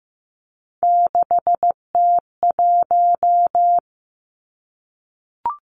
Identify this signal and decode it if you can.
Morse code, heard as a keyed tone, about 15 wpm, 700 Hz. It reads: 6T1
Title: Three Challenging Letter-Number Combinations 15wpm